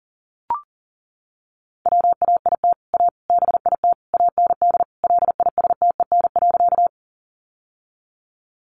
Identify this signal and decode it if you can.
Morse code, heard as a keyed tone, about 40 wpm, 700 Hz. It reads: WAIT A BIT AND LISTEN.
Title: Sentences from Top 900 Words 40wpm